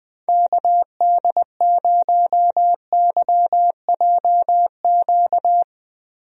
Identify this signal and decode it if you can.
Morse code, heard as a keyed tone, about 20 wpm, 700 Hz. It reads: KD0YJQ
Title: USA Call Signs - 6 Characters - Repeated 3x - 20wpm